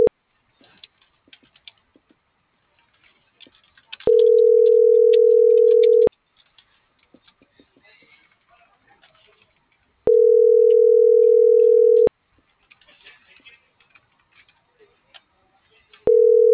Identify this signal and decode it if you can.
Ringback tone (North America), 440 + 480 Hz.